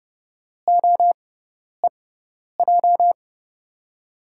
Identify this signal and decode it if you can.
Morse code, heard as a keyed tone, about 32 wpm, spaced as for 10 wpm, 700 Hz. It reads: OEJ
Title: USA Call Signs - 6 Characters - 6x Character Spacing - 30wpm